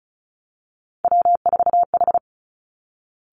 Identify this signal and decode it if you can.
Morse code, heard as a keyed tone, about 36 wpm, 700 Hz. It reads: W4H
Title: Three Challenging Letter-Number Combinations 35wpm